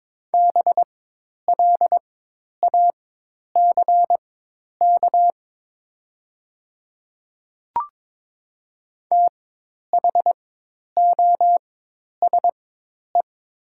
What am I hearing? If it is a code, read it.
Morse code, heard as a keyed tone, about 22 wpm, spaced as for 10 wpm, 700 Hz. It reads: BLACK THOSE
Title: 4x Character Spacing - Top 500 Words - 22wpm